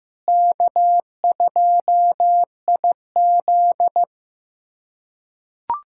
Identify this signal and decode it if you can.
Morse code, heard as a keyed tone, about 16 wpm, 700 Hz. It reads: K2IZ
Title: K1USN SST QSO Practice - 15wpm